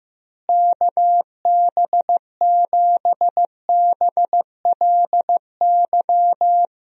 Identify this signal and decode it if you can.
Morse code, heard as a keyed tone, about 15 wpm, 700 Hz. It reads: KB7BLY
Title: USA Call Signs - 6 Characters - Repeated 3x - 15wpm